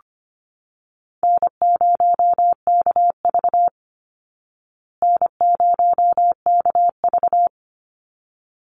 Morse code, heard as a keyed tone, about 25 wpm, 700 Hz. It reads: N0KV N0KV
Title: Call Signs Repeated 3x - 25wpm